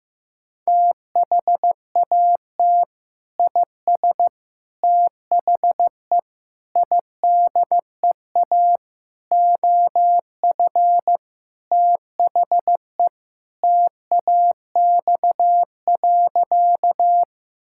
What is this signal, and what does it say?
Morse code, heard as a keyed tone, about 15 wpm, 700 Hz. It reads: THAT IS THE IDEA OF THE TAX.
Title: Sentences from Top 1000 Words - Review Part 5 - 15wpm